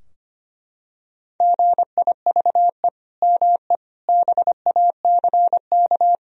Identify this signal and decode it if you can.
Morse code, heard as a keyed tone, about 25 wpm, 700 Hz. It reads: GIVE ME BACK
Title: Sentences from Top 1000 Words - Review Part 4 - 25wpm